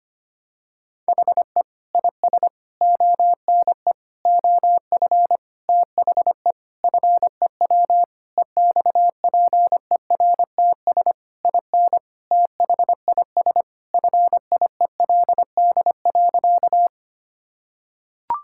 Morse code, heard as a keyed tone, about 25 wpm, 700 Hz. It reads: HE IS ONE OF THE FEW EXPERTS IN THIS FIELD.
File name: practice / 25wpm / Sentences with 4-Word Phrases - 25wpm